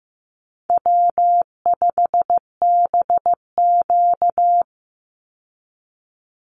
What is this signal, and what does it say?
Morse code, heard as a keyed tone, about 15 wpm, 700 Hz. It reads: W5BQ
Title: Easy Call Signs Repeated 2x - 15wpm